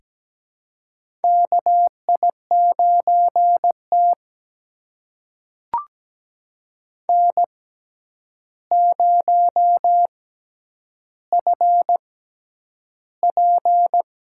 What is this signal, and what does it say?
Morse code, heard as a keyed tone, about 17 wpm, spaced as for 9 wpm, 700 Hz. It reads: KI9T N0FP
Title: USA Call Signs - 4 Characters - 6x Character Spacing - 17wpm